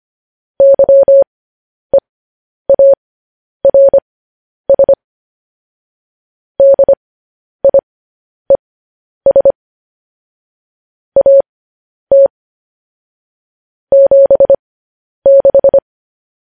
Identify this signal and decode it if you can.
Morse code, heard as a keyed tone, about 25 wpm, spaced as for 10 wpm, 550 Hz. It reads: YEARS DIES AT 76